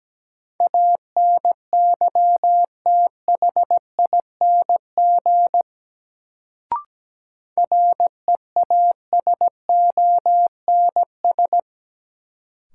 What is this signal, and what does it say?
Morse code, heard as a keyed tone, about 17 wpm, 700 Hz. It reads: ANYTHING REASONS
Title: Sets of 4 Words - New Word Drill - 17wpm